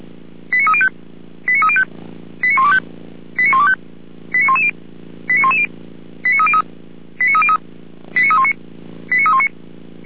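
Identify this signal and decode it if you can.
Five-tone selective call (ZVEI), heard as quick runs of five tones, tones of about 70 ms. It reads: 89305 89305 89125 89125 89110 89110 89303 89303 89319 89319
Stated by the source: ZVEI five-tone selective call received on SDR.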